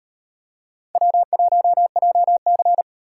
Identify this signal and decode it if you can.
Morse code, heard as a keyed tone, about 36 wpm, 700 Hz. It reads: W1JC